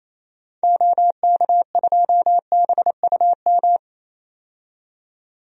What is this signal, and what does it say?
Morse code, heard as a keyed tone, about 28 wpm, 700 Hz. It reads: OK2BUM
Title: DXCC-100 Call Signs - 6 Characters - 28wpm